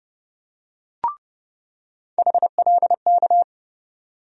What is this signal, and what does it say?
Morse code, heard as a keyed tone, about 30 wpm, 700 Hz. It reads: HLK
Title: USA Call Sign Postfixes - 3 Letters 30wpm